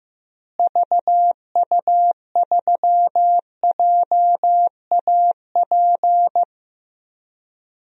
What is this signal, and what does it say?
Morse code, heard as a keyed tone, about 15 wpm, 700 Hz. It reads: VU3JAP
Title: DXCC-100 Call Signs - 6 Characters - Repeated 3x - 15wpm